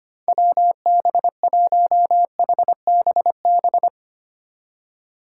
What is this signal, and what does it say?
Morse code, heard as a keyed tone, about 33 wpm, 700 Hz. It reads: WB1HBB